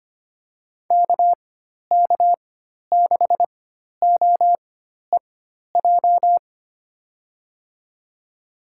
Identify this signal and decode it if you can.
Morse code, heard as a keyed tone, about 25 wpm, spaced as for 12 wpm, 700 Hz. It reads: KK6OEJ
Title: USA Call Signs - 4x Character Spacing - 25wpm